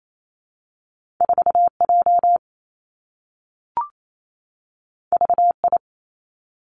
Morse code, heard as a keyed tone, about 28 wpm, 700 Hz. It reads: 4 J V I
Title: Two Challenging Letter-Number Combinations 28wpm